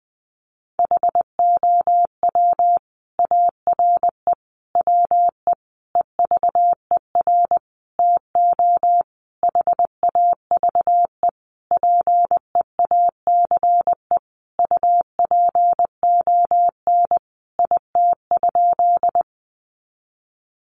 Morse code, heard as a keyed tone, about 20 wpm, 700 Hz. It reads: HOW ARE WE EVER TO HAVE PEACE UPON IT?
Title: Sentences from Top 1500 Words 20wpm